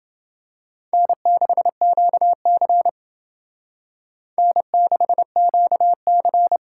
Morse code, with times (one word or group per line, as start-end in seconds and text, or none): 0.9-2.9 N6QC
4.4-6.6 N6QC